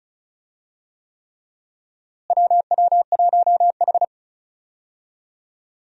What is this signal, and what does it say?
Morse code, heard as a keyed tone, about 36 wpm, 700 Hz. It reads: WW1H
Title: USA Call Signs - 4 Characters - Repeated 3x - 35wpm